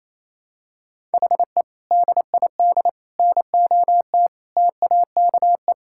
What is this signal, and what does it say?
Morse code, heard as a keyed tone, about 27 wpm, 700 Hz. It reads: HE DID NOT TAKE
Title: Sentences from Top 1000 Words - Review Part 6 - 28wpm